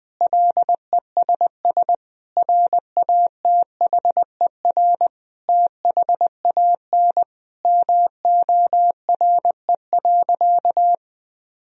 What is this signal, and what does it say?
Morse code, heard as a keyed tone, about 21 wpm, 700 Hz. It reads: LESS RATHER THAN MORE.
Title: Sentences from Top 1000 Words - Review Part 4 - 20wpm